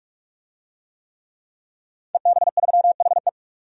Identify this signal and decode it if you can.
Morse code, heard as a keyed tone, about 43 wpm, 700 Hz. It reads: EB3HE
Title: DXCC-100 Call Signs - 5 Characters - 45wpm